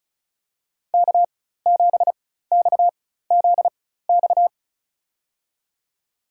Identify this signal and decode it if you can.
Morse code, heard as a keyed tone, about 34 wpm, spaced as for 16 wpm, 700 Hz. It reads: K7XZX
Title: USA Call Signs - 5 Characters - 4x Character Spacing - 35wpm